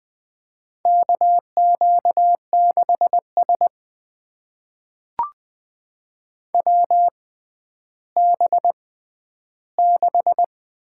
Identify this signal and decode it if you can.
Morse code, heard as a keyed tone, about 20 wpm, spaced as for 11 wpm, 700 Hz. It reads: KQ6S WB6
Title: USA Call Signs - 6x Character Spacing - 20wpm